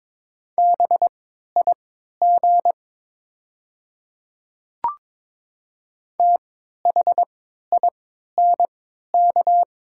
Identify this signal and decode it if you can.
Morse code, heard as a keyed tone, about 22 wpm, spaced as for 13 wpm, 700 Hz. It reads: BIG THINK
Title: 3x Character Spacing - Top 200 Words - 22wpm